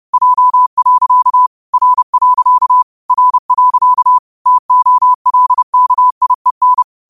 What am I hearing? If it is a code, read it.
Morse code, heard as a keyed tone, about 30 wpm, 1000 Hz. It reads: J. R. R. TOLKIEN
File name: qotdfiles / 30wpm